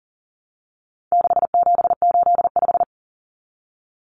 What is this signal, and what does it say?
Morse code, heard as a keyed tone, about 40 wpm, 700 Hz. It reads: 6785
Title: Common Cut Numbers - 4-Digit 40wpm